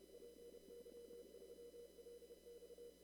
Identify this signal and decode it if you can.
Morse code, heard as a keyed tone, about 31 wpm, 480 Hz, with noise. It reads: PY3LX
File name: CW / kt5j wpx cw-op-k5pi